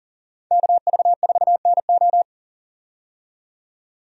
Morse code, heard as a keyed tone, about 40 wpm, 700 Hz. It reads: KV4NO
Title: USA Call Signs - 5 Characters - 40wpm